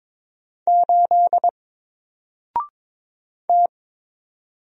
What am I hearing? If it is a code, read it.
Morse code, heard as a keyed tone, about 22 wpm, 700 Hz. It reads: 8 T
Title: Single Letter-Number Alt 22wpm